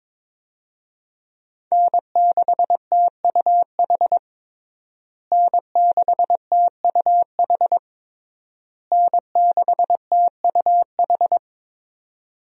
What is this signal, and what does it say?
Morse code, heard as a keyed tone, about 22 wpm, 700 Hz. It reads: N6TUH N6TUH N6TUH
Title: USA Call Signs - 5 Characters - Repeated 3x - 22wpm